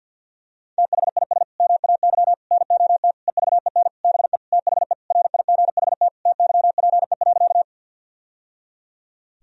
Morse code, heard as a keyed tone, about 50 wpm, 700 Hz. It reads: THIS MAY NOT EVEN BE THE RIGHT TYPE.
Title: Sentences from Top 1000 Words - Review Part 3 - 50wpm